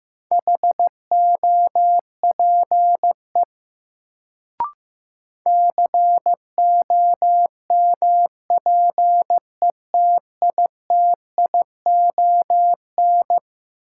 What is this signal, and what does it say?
Morse code, heard as a keyed tone, about 15 wpm, 700 Hz. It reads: HOPE COMPETITION
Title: Sentences from Top 1000 Words - New Word Drill - 15wpm